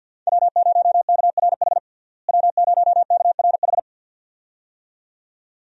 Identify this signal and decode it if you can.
Morse code, heard as a keyed tone, about 52 wpm, 700 Hz. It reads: W0KRH W0KRH
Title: USA Call Signs - 5 Characters - Repeated 2x - 50wpm